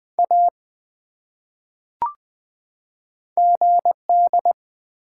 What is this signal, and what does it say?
Morse code, heard as a keyed tone, about 20 wpm, 700 Hz. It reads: A GD
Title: Top 180 QSO Elements 20wpm